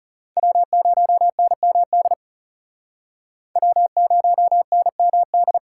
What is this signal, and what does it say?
Morse code, heard as a keyed tone, about 39 wpm, 700 Hz. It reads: W0NMD W0NMD